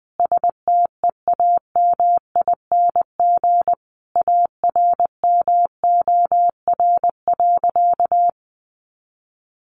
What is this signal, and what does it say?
Morse code, heard as a keyed tone, about 20 wpm, 700 Hz. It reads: STEAMING ARMOR.
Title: A Sound Of Thunder 20wpm